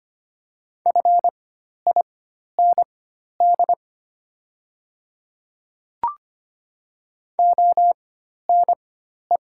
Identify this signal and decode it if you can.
Morse code, heard as a keyed tone, about 25 wpm, spaced as for 12 wpm, 700 Hz. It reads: FIND ONE
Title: 4x Character Spacing - Top 100 Words - 25wpm